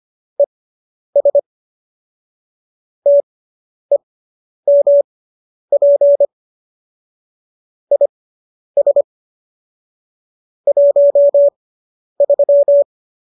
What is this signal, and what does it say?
Morse code, heard as a keyed tone, about 25 wpm, spaced as for 10 wpm, 570 Hz. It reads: ES TEMP IS 13